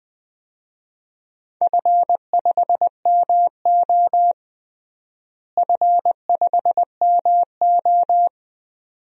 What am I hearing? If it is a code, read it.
Morse code, heard as a keyed tone, about 20 wpm, 700 Hz. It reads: F5MO F5MO
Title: DXCC-100 Call Signs - 4 Characters - Repeated 2x - 20wpm